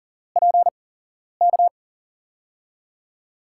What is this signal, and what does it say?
Morse code, heard as a keyed tone, about 40 wpm, spaced as for 11 wpm, 700 Hz. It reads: PK
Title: USA Call Signs - 5 Characters - 8x Character Spacing - 40wpm